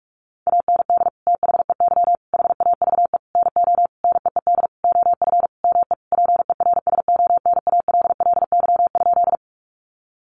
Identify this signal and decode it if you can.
Morse code, heard as a keyed tone, about 44 wpm, 700 Hz. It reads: AND THEY HAVE NO NEED OF ME PERSONALLY?
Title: Sentences from Top 1400 Words 45wpm